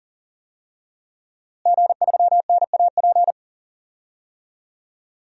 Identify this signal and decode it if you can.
Morse code, heard as a keyed tone, about 40 wpm, 700 Hz. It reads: G3NAP
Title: DXCC-100 Call Signs - 5 Characters - Repeated 3x - 40wpm